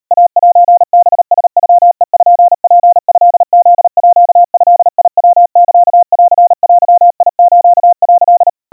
Morse code (Japanese Wabun code), heard as a keyed tone, about 37 wpm, 700 Hz. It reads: イセハラノペツトフードヤサンデス」
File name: kawa / Morse sample / kamyu 140 cpm 700 Hz Crispy